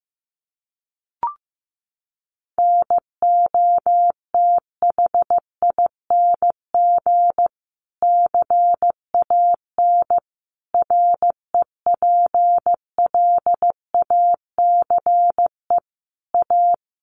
Morse code, heard as a keyed tone, about 14 wpm, 700 Hz. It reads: NOTHING CAN REPLACE A
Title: Sentences from Top 1400 Words 15wpm